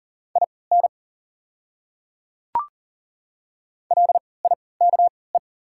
Morse code, heard as a keyed tone, about 40 wpm, spaced as for 23 wpm, 700 Hz. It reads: IN LIKE
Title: 3x Character Spacing - Top 100 Words - 40wpm